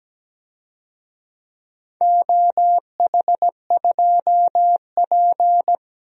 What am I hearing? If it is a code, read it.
Morse code, heard as a keyed tone, about 17 wpm, 700 Hz. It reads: OH2P
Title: DXCC-100 Call Signs - 4 Characters - Repeated 3x - 17wpm